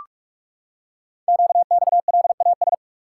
Morse code, heard as a keyed tone, about 44 wpm, 700 Hz. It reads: QXLAS
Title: Code Groups - 5 Letters - 45wpm